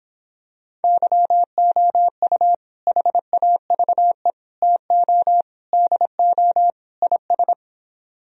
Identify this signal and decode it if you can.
Morse code, heard as a keyed tone, about 26 wpm, 700 Hz. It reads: YOU HAVE TO DO IS